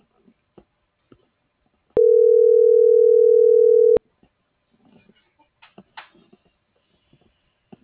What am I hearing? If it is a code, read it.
Ringback tone (North America), 440 + 480 Hz.